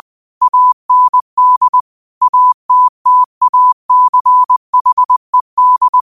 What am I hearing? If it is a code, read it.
Morse code, heard as a keyed tone, about 20 wpm, 1000 Hz. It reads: AND ATTACHED